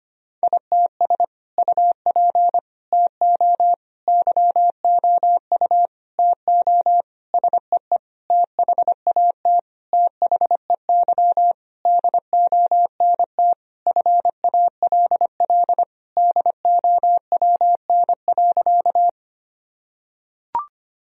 Morse code, heard as a keyed tone, about 25 wpm, 700 Hz. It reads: ITS UP TO YOU TO SEE THAT THEY DONT FALL DOWN.
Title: Sentences from Top 1000 Words - Review Part 2 - 25wpm